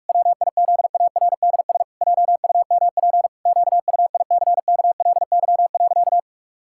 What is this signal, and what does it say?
Morse code, heard as a keyed tone, about 46 wpm, 700 Hz. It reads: WIZARDS JUMP QUICKLY.
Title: Pangram Challenge v2 - 45wpm